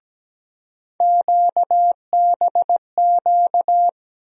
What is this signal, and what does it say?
Morse code, heard as a keyed tone, about 17 wpm, 700 Hz. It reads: QBQ
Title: USA Call Sign Postfixes - 3 Letters 17wpm